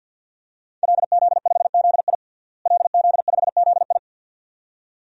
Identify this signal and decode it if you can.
Morse code, heard as a keyed tone, about 47 wpm, 700 Hz. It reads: LZ5ZI LZ5ZI
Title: DXCC-100 Call Signs - 5 Characters - Repeated 2x - 50wpm